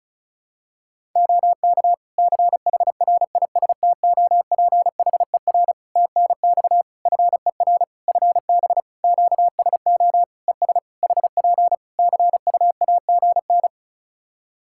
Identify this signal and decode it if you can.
Morse code, heard as a keyed tone, about 36 wpm, 700 Hz. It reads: OK CHRISTOPHER TNX FER FB QSO ES HP CUAGN
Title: Practice QSO - Exchange 4 - Ending - 35wpm